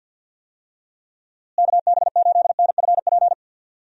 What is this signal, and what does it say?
Morse code, heard as a keyed tone, about 50 wpm, 700 Hz. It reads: KB8NFP
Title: USA Call Signs - 6 Characters - Repeated 2x - 50wpm